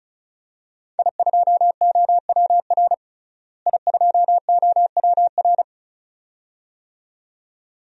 Morse code, heard as a keyed tone, about 36 wpm, 700 Hz. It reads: I2OWR I2OWR
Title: DXCC-100 Call Signs - 5 Characters - Repeated 2x - 35wpm